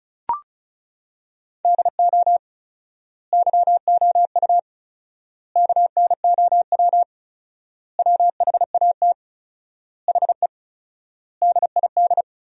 Morse code, heard as a keyed tone, about 34 wpm, 700 Hz. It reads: DO YOU KNOW WHAT HE DID
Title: 4x Word-Spacing - Sentences from Top 300 Words - 35wpm